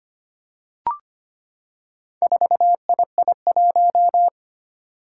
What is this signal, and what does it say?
Morse code, heard as a keyed tone, about 25 wpm, 700 Hz. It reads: 4II1
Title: Four Challenging Letter-Number Combinations 25wpm